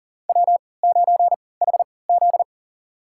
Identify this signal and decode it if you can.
Morse code, heard as a keyed tone, about 40 wpm, spaced as for 23 wpm, 700 Hz. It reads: W9HZ
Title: USA Call Signs - 4 Characters - 3x Character Spacing - 40wpm